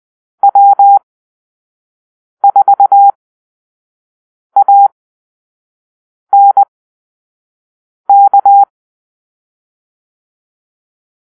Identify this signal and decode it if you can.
Morse code, heard as a keyed tone, about 20 wpm, spaced as for 5 wpm, 800 Hz. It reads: W4ANK
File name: arrl / Call-Sign Drill-4